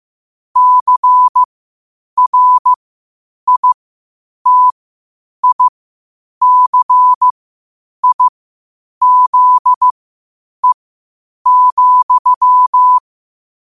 Morse code, heard as a keyed tone, about 15 wpm, spaced as for 8 wpm, 1000 Hz. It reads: CRITICIZE,